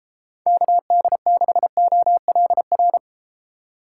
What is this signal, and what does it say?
Morse code, heard as a keyed tone, about 33 wpm, 700 Hz. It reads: KD6OLR